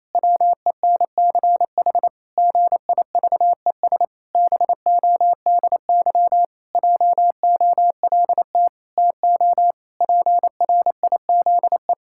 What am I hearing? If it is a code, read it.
Morse code, heard as a keyed tone, about 28 wpm, 700 Hz. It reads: WENCH GIVES BODY JOLT TO PRIZE
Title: Pangram Challenge v2 - 28wpm